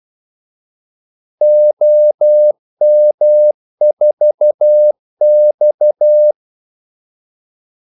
Morse code, heard as a keyed tone, about 12 wpm, 600 Hz. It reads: OM4X
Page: 2 ekstra ordmellemrum mellem kaldesignalerne